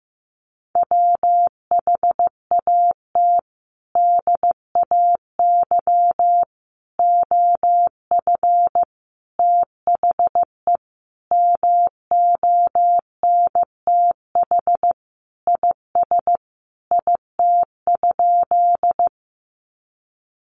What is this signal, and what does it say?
Morse code, heard as a keyed tone, about 15 wpm, 700 Hz. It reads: WHAT DAY OF THE MONTH IS IT?
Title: Sentences from Top 800 Words 15wpm